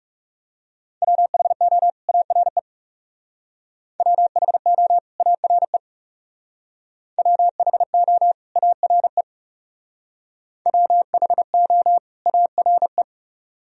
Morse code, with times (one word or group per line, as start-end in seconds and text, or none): 1.0-1.9 WHO
2.1-2.6 ARE
4.0-5.0 WHO
5.2-5.8 ARE
7.2-8.3 WHO
8.6-9.2 ARE
10.7-12.0 WHO
12.3-13.0 ARE